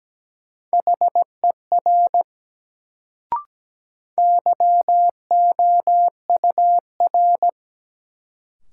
Morse code, heard as a keyed tone, about 17 wpm, 700 Hz. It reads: HER YOUR
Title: Sentences from Top 100 Words - New Word Drill - 17wpm